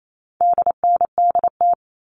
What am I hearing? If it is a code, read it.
Morse code, heard as a keyed tone, about 28 wpm, 700 Hz. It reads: DNDT
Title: All Cut Numbers - 4-Digit 28wpm